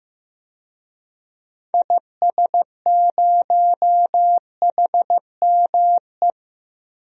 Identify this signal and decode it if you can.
Morse code, heard as a keyed tone, about 16 wpm, 690 Hz. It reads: IS0HME
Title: DXCC-100 Call Signs - 6 Characters - 15wpm